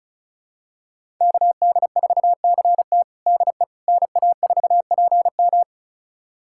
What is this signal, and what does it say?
Morse code, heard as a keyed tone, about 34 wpm, 700 Hz. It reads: KD4CT DE NA4PM
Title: USA Callsign DE Callsign - 5 Characters - 35wpm